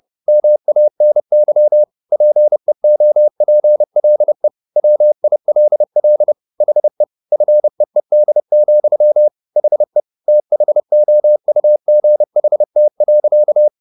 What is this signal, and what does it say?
Morse code, heard as a keyed tone, about 30 wpm, 600 Hz. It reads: MANY PEOPLE WILL HE FEED, HE THOUGHT.